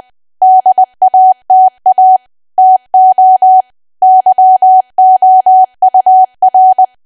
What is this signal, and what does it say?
Morse code, heard as a keyed tone, about 20 wpm, 750 Hz. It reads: DATA TO YOUR
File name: cwa / W1AW-Level3 / 20WPM